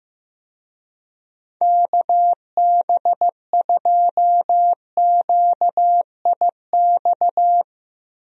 Morse code, heard as a keyed tone, about 15 wpm, 700 Hz. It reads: KB2QIX
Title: USA Call Signs - 6 Characters - Repeated 3x - 15wpm